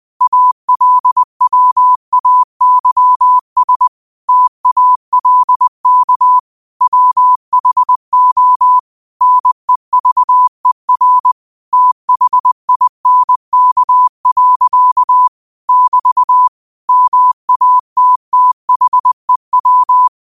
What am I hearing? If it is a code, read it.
Morse code, heard as a keyed tone, about 20 wpm, 1000 Hz. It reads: ALWAYS TALK WHO NEVER THINK. <BT> MATTHEW
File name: qotdfiles / 20wpm